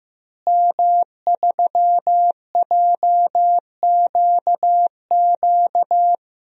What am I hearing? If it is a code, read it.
Morse code, heard as a keyed tone, about 16 wpm, 700 Hz. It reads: M3JQQ
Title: DXCC-100 Call Signs - 5 Characters - Repeated 3x - 15wpm